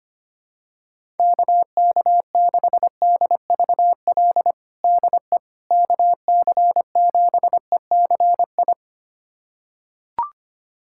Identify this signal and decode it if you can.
Morse code, heard as a keyed tone, about 25 wpm, 700 Hz. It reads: KK6DVL DE KC7ECI
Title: USA Callsign DE Callsign - 6 Characters - 25wpm